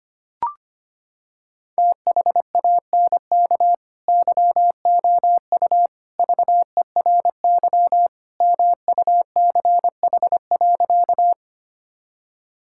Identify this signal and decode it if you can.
Morse code, heard as a keyed tone, about 25 wpm, 700 Hz. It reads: THANK YOU VERY MUCH.
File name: practice / 25wpm / Sentences with 3-Word Phrases - 25wpm